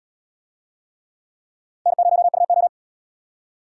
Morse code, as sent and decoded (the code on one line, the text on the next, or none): -. ----- ..- ---
N0UO